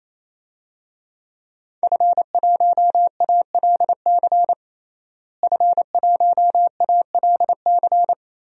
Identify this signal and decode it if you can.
Morse code, heard as a keyed tone, about 28 wpm, 690 Hz. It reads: F1ALC F1ALC
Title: DXCC-100 Call Signs - 5 Characters - Repeated 3x - 28wpm